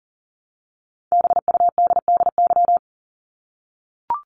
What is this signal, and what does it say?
Morse code, heard as a keyed tone, about 40 wpm, 700 Hz. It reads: BUDDY
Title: Top 1751-2000 Words 40wpm